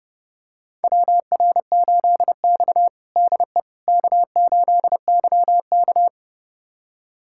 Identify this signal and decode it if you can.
Morse code, heard as a keyed tone, about 30 wpm, 700 Hz. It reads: WR8X DE K8YK
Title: USA Callsign DE Callsign - 4 Characters - 30wpm